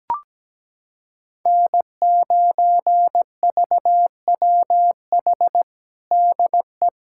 Morse code, heard as a keyed tone, about 17 wpm, 700 Hz. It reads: N9VWH DE
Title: USA Callsign DE Callsign - 5 Characters - 17wpm